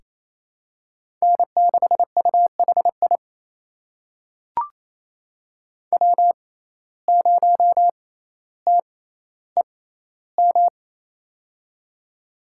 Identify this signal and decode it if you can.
Morse code, heard as a keyed tone, about 28 wpm, spaced as for 14 wpm, 700 Hz. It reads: N6UHI W0TEM
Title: USA Call Signs - 5 Characters - 6x Character Spacing - 28wpm